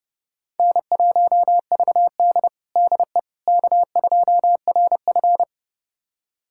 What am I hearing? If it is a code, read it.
Morse code, heard as a keyed tone, about 32 wpm, 700 Hz. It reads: N1VD DE K2RF
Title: USA Callsign DE Callsign - 4 Characters - 30wpm